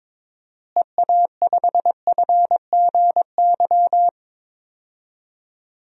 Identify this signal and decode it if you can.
Morse code, heard as a keyed tone, about 22 wpm, 700 Hz. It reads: EA5FGY
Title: DXCC-100 Call Signs - 6 Characters - Repeated 2x - 22wpm